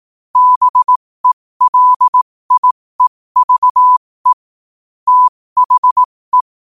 Morse code, heard as a keyed tone, about 18 wpm, spaced as for 15 wpm, 1000 Hz. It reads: BELIEVE THE
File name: qotdfiles / 15wpm